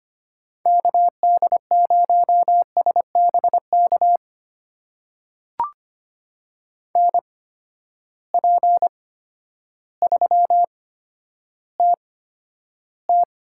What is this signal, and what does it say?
Morse code, heard as a keyed tone, about 25 wpm, spaced as for 11 wpm, 700 Hz. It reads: KD0SBK NP3TT
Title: USA Call Signs - 8x Character Spacing - 25wpm